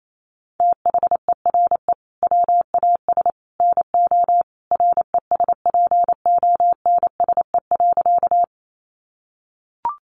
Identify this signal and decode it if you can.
Morse code, heard as a keyed tone, about 28 wpm, 700 Hz. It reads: THERE WAS NO RESPONSE.
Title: Sentences from Top 1400 Words 28wpm